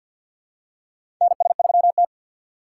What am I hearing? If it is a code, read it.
Morse code, heard as a keyed tone, about 53 wpm, 700 Hz. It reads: NS3T
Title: USA Call Signs - 3x Character Spacing - 50wpm